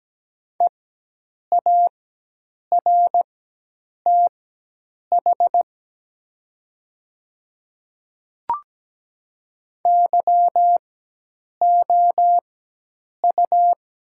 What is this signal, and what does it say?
Morse code, heard as a keyed tone, about 17 wpm, spaced as for 8 wpm, 700 Hz. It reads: EARTH YOU
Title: 4x Character Spacing - Top 200 Words - 17wpm